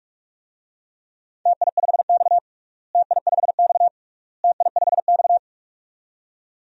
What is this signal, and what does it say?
Morse code, heard as a keyed tone, about 41 wpm, 700 Hz. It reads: TI5X TI5X TI5X
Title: DXCC-100 Call Signs - 4 Characters - Repeated 3x - 45wpm